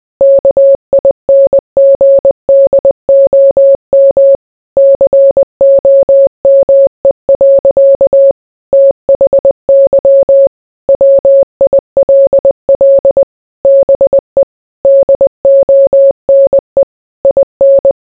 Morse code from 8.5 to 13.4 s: THY WILL